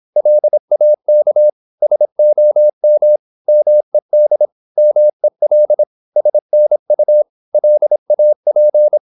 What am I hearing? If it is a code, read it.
Morse code, heard as a keyed tone, about 26 wpm, 600 Hz. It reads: LAK SOM MED MEL SNU LAP
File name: Korte ord DK 26wpm